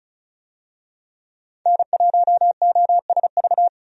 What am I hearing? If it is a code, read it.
Morse code, heard as a keyed tone, about 36 wpm, 700 Hz. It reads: N1OSV
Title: USA Call Signs - 5 Characters - 35wpm